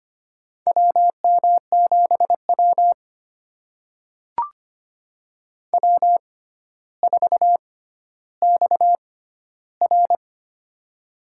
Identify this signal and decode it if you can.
Morse code, heard as a keyed tone, about 25 wpm, spaced as for 13 wpm, 700 Hz. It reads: WM7W W4XR
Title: USA Call Signs - 4 Characters - 6x Character Spacing - 25wpm